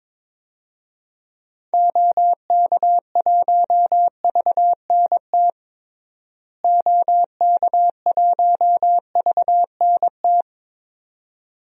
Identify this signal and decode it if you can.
Morse code, heard as a keyed tone, about 22 wpm, 700 Hz. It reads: OK1VNT OK1VNT
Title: DXCC-100 Call Signs - 6 Characters - Repeated 2x - 22wpm